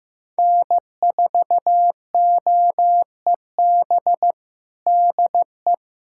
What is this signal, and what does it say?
Morse code, heard as a keyed tone, about 16 wpm, 700 Hz. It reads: N4OEB DE
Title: USA Callsign DE Callsign 15wpm